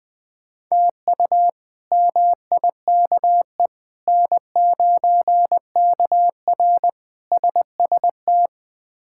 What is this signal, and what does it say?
Morse code, heard as a keyed tone, about 20 wpm, 700 Hz. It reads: TU MIKE N9KR SST
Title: K1USN SST QSO Practice - 20wpm